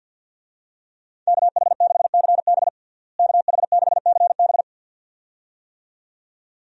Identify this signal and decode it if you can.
Morse code, heard as a keyed tone, about 50 wpm, 700 Hz. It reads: KH6CB KH6CB
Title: USA Call Signs - 5 Characters - Repeated 2x - 50wpm